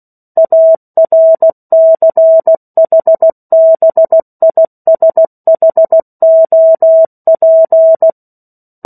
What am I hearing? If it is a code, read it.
Morse code, heard as a keyed tone, about 16 wpm, 650 Hz. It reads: ARCHBISHOP
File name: 03-Chapter-03 16 WPM